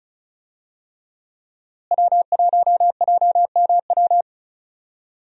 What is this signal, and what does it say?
Morse code, heard as a keyed tone, about 34 wpm, 700 Hz. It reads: W1JMW
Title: USA Call Signs - 5 Characters - Repeated 3x - 35wpm